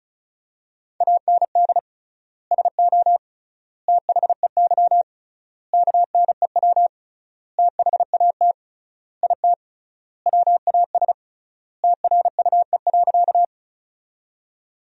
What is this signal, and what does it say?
Morse code, heard as a keyed tone, about 36 wpm, 700 Hz. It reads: AND SO THEY KNEW THAT IT WAS TRUE.
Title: 3x Word-Spacing - Sentences from Top 400 Words - 35wpm